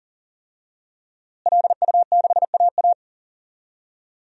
Morse code, heard as a keyed tone, about 40 wpm, 700 Hz. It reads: LU6AA
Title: DXCC-100 Call Signs - 5 Characters - Repeated 3x - 40wpm